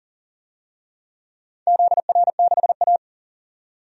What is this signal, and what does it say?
Morse code, heard as a keyed tone, about 40 wpm, 700 Hz. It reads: ZR6A
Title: DXCC-100 Call Signs - 4 Characters - 40wpm